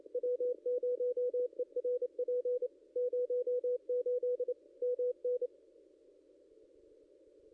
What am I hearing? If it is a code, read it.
Morse code, heard as a keyed tone, about 28 wpm, 490 Hz. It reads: W0ERP 08 MN
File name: contests / ss-cw-k5tr-op-n5rz -1600